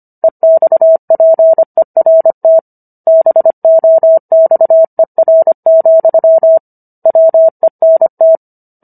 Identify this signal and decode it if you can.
Morse code, heard as a keyed tone, about 25 wpm, 650 Hz. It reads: EXPERT BOXER, WENT